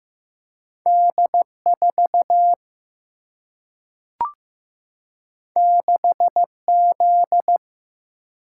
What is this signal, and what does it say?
Morse code, heard as a keyed tone, about 15 wpm, 700 Hz. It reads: D4 6Z
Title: Single-Letter Single-Number 15wpm